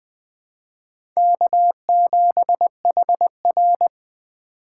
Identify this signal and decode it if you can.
Morse code, heard as a keyed tone, about 20 wpm, 700 Hz. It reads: K7HR
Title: USA Call Signs - 4 Characters - 3x Character Spacing - 20wpm